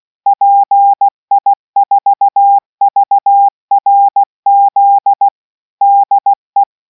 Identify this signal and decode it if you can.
Morse code, heard as a keyed tone, about 16 wpm, 800 Hz. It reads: PI4VRZ DE